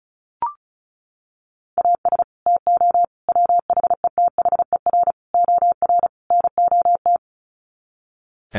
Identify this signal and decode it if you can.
Morse code, heard as a keyed tone, about 34 wpm, 700 Hz. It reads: AS TO WHETHER OR NOT
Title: Sets of 5 Words Encore 35wpm